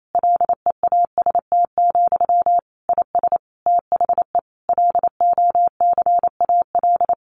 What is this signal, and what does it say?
Morse code, heard as a keyed tone, about 28 wpm, 700 Hz. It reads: LEAST, IS THE LOCAL
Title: Sentences from Top 1500 Words 28wpm